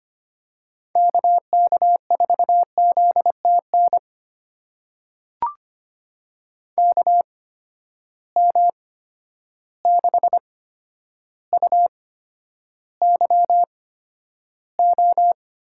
Morse code, heard as a keyed tone, about 25 wpm, spaced as for 11 wpm, 700 Hz. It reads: KK4ZTN KM6UYO
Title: USA Call Signs - 6 Characters - 8x Character Spacing - 25wpm